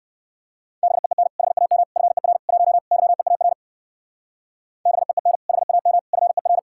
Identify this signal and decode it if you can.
Morse code, heard as a keyed tone, about 56 wpm, 700 Hz. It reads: BEEN HAM FER 1 YEAR BEEN HAM FER